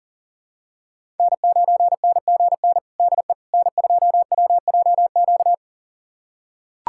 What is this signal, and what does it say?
Morse code, heard as a keyed tone, about 37 wpm, 700 Hz. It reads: N9NGN DE N2WJQ
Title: USA Callsign DE Callsign 40wpm